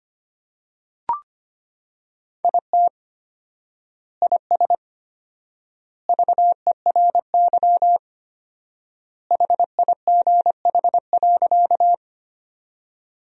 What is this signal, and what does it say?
Morse code, heard as a keyed tone, about 25 wpm, 700 Hz. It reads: IT IS VERY HIGH.
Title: 4x Word-Spacing - Sentences from Top 200 Words - 25wpm